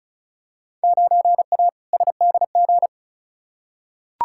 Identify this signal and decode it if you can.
Morse code, heard as a keyed tone, about 36 wpm, 700 Hz. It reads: 9A SDG
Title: Field Day Exchanges Encore - 35wpm